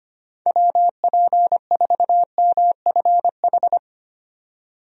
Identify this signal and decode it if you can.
Morse code, heard as a keyed tone, about 25 wpm, 700 Hz. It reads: WP4MFH
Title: USA Call Signs - 6 Characters - 25wpm